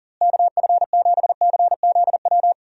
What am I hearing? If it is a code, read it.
Morse code, heard as a keyed tone, about 40 wpm, 700 Hz. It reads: KF7CZW
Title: USA Call Signs - 6 Characters - Repeated 3x - 40wpm